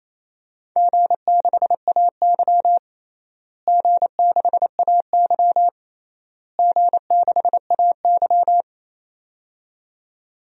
Morse code, heard as a keyed tone, about 27 wpm, 700 Hz. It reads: G6AY G6AY G6AY
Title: DXCC-100 Call Signs - 4 Characters - Repeated 3x - 28wpm